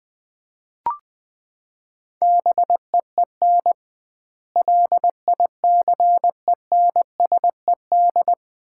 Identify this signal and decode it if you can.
Morse code, heard as a keyed tone, about 20 wpm, 700 Hz. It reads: BEEN LICENSED
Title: Common QSO Phrases - 2x Word-Spacing - 20wpm